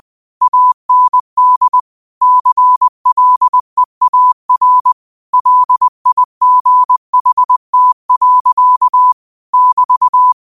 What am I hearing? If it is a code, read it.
Morse code, heard as a keyed tone, about 20 wpm, 1000 Hz. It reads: AND CLEAR LIGHT. <BT>
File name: qotdfiles / 20wpm